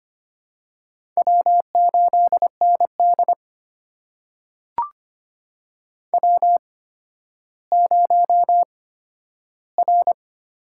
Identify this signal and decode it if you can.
Morse code, heard as a keyed tone, about 25 wpm, spaced as for 12 wpm, 700 Hz. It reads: W8ND W0R
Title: USA Call Signs - 8x Character Spacing - 25wpm